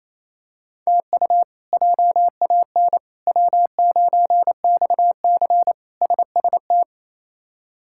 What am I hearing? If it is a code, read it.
Morse code, heard as a keyed tone, about 27 wpm, 700 Hz. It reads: TU JAN W9XC SST
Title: K1USN SST QSO Practice - 28wpm